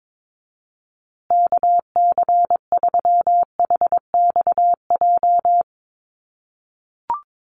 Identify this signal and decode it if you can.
Morse code, heard as a keyed tone, about 22 wpm, 700 Hz. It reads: KC3HXJ
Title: USA Call Signs Repeated 3x 22wpm